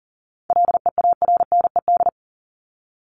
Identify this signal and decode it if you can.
Morse code, heard as a keyed tone, about 40 wpm, 700 Hz. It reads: LEARNED
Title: ED Suffix - Top 100 Words with Base Words 40wpm